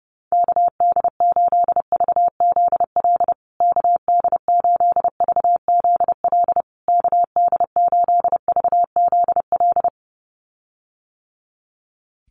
Morse code, heard as a keyed tone, about 30 wpm, 700 Hz. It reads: KD8VZL KD8VZL KD8VZL
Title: USA Call Signs Repeated 3x 30wpm